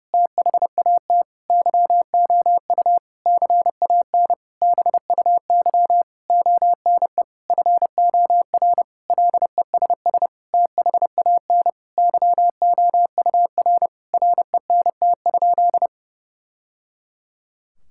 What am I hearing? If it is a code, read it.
Morse code, heard as a keyed tone, about 30 wpm, 700 Hz. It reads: THAT YOU CAN BUY ONE FOR LESS THAN YOUR RENT?
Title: Sentences from Top 1000 Words - Review Part 5 - 30wpm